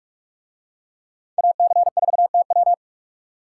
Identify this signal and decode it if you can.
Morse code, heard as a keyed tone, about 44 wpm, 700 Hz. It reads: AC4TW